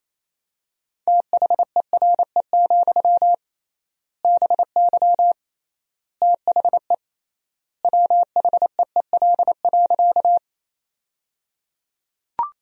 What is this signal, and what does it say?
Morse code, heard as a keyed tone, about 28 wpm, 700 Hz. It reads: THERE, BY THE WHEEL.
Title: 3x Word-Spacing - Sentences from Top 500 Words - 28wpm